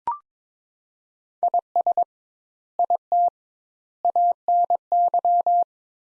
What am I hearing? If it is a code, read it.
Morse code, heard as a keyed tone, about 22 wpm, 700 Hz. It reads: IS IT ANY
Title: Sentences Top 200 Words - 22wpm